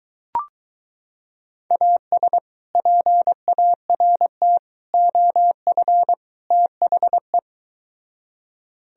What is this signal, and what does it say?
Morse code, heard as a keyed tone, about 23 wpm, 700 Hz. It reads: AS PART OF THE